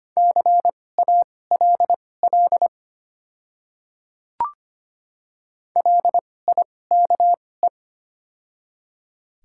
Morse code, heard as a keyed tone, about 25 wpm, spaced as for 18 wpm, 690 Hz. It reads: CALL LIKE
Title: 2x Character Spacing - Top 100 Words - 25wpm